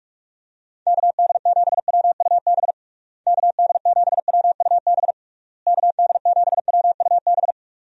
Morse code, heard as a keyed tone, about 41 wpm, 700 Hz. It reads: KD7WUB KD7WUB KD7WUB
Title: USA Call Signs - 6 Characters - Repeated 3x - 45wpm